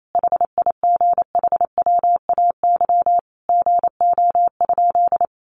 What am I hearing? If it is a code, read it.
Morse code, heard as a keyed tone, about 28 wpm, 700 Hz. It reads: HIGHWAY GO?
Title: Sentences from Top 1400 Words 28wpm